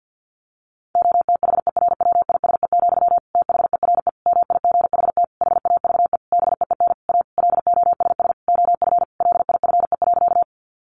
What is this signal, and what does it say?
Morse code, heard as a keyed tone, about 50 wpm, 700 Hz. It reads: OTHERWISE, THERE MIGHT HAVE BEEN A LOSS OF LIFE.
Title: Sentences from Top 1400 Words 50wpm